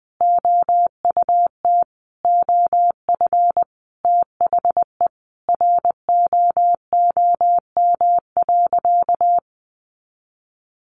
Morse code, heard as a keyed tone, about 20 wpm, 700 Hz. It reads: OUT OF THE ROOM.
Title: Sentences from Top 500 Words 20wpm